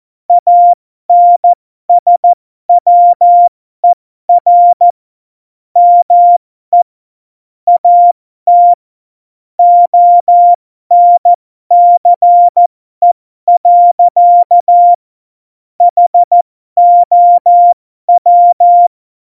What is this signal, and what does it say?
Morse code, heard as a keyed tone, about 14 wpm, spaced as for 12 wpm, 700 Hz. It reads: ANSWER ME AT ONCE. HOW
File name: Morsekurs / Witze